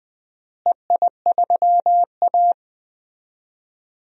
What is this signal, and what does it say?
Morse code, heard as a keyed tone, about 20 wpm, 700 Hz. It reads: EI3A
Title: DXCC-100 Call Signs - 4 Characters - Repeated 2x - 20wpm